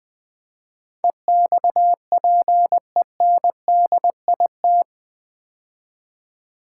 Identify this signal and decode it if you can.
Morse code, heard as a keyed tone, about 20 wpm, 700 Hz. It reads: EXPENDIT
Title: Build A Word - 4 Letters At A Time - 11-Letter Words - 20wpm